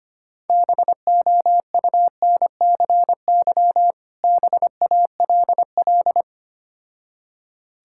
Morse code, heard as a keyed tone, about 25 wpm, 700 Hz. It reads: BOUNCY BALL
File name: practice / 25wpm / Difficult Words with Hints - 25wpm